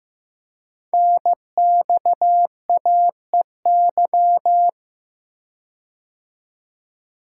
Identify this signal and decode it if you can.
Morse code, heard as a keyed tone, about 15 wpm, 700 Hz. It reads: NXAEY
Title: Code Groups - 5 Letters - 15wpm